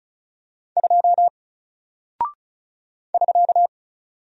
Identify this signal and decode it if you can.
Morse code, heard as a keyed tone, about 36 wpm, 700 Hz. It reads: 2 <SK>
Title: Single Character (Letter:Number:Punctuation:Prosign) Alt 35wpm